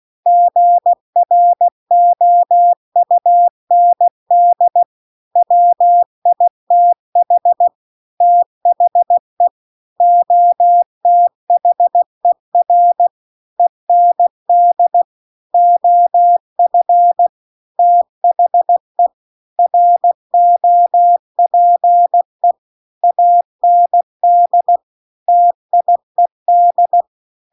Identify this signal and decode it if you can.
Morse code, heard as a keyed tone, about 16 wpm, 700 Hz. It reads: GROUND WITH THE OTHER END OF THE ROPE AND TIED